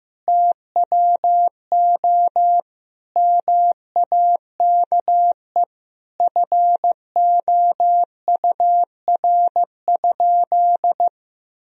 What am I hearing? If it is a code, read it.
Morse code, heard as a keyed tone, about 16 wpm, 700 Hz. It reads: TWO MAKE FOUR?
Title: Sentences from Top 1000 Words - Review Part 1 - 15wpm